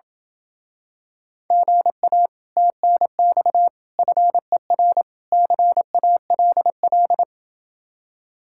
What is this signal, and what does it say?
Morse code, heard as a keyed tone, about 28 wpm, 700 Hz. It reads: GA TNX FER CALL